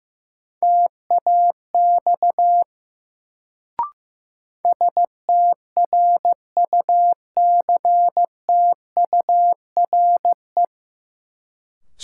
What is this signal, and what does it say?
Morse code, heard as a keyed tone, about 15 wpm, 700 Hz. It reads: TAX STRUCTURE
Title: Sentences from Top 1500 Words - New Word Drill Alt - 15wpm